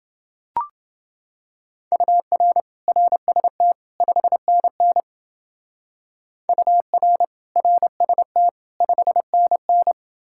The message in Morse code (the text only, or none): UR RST 5NN UR RST 5NN